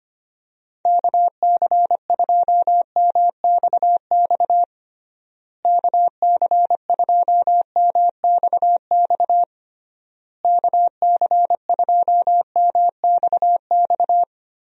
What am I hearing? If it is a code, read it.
Morse code, heard as a keyed tone, about 25 wpm, 700 Hz. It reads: KC2MXX KC2MXX KC2MXX
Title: USA Call Signs - 6 Characters - Repeated 3x - 25wpm